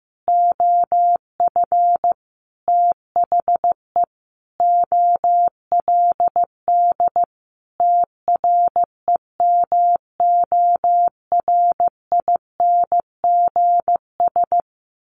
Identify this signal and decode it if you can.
Morse code, heard as a keyed tone, about 15 wpm, 700 Hz. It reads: OF THE OLD TREMORINGS
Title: A Sound Of Thunder 15wpm